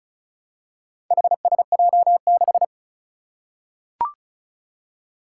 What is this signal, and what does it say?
Morse code, heard as a keyed tone, about 36 wpm, 700 Hz. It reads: HSJ6
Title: Four Challenging Letter-Number Combinations 35wpm